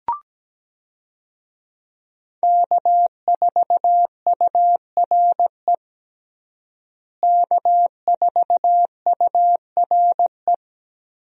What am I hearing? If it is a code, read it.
Morse code, heard as a keyed tone, about 17 wpm, 700 Hz. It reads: K4URE K4URE
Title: USA Call Signs - 5 Characters - Repeated 3x - 17wpm